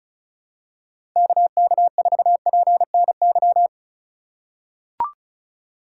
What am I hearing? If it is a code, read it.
Morse code, heard as a keyed tone, about 36 wpm, 700 Hz. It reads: KK4PNY
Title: USA Call Signs - 4x Character Spacing - 35wpm